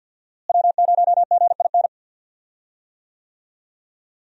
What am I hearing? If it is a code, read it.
Morse code, heard as a keyed tone, about 51 wpm, 700 Hz. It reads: W0GIN
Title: USA Call Signs - 5 Characters - Repeated 2x - 50wpm